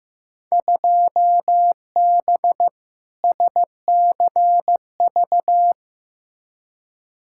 Morse code, heard as a keyed tone, about 15 wpm, 700 Hz. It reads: 2B SCV
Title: Field Day Exchanges Encore - 15wpm